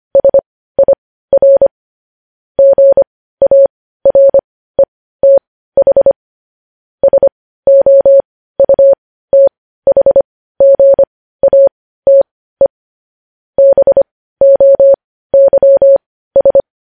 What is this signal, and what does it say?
Morse code, heard as a keyed tone, about 25 wpm, spaced as for 15 wpm, 550 Hz. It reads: SIR GARETH SOUTHGATE BOYS
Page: news-headlines-in-morse-code-at-15-wpm